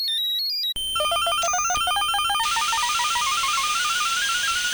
There is no code.